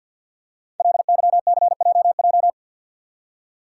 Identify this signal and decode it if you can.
Morse code, heard as a keyed tone, about 51 wpm, 700 Hz. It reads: LYCJJ
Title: Code Groups - 5 Letter:Numbers - 50wpm